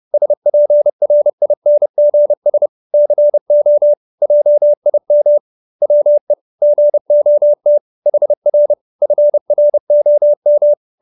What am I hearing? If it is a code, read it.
Morse code, heard as a keyed tone, about 30 wpm, 590 Hz. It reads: SPRINGS CO JIM WE GOT HR FROM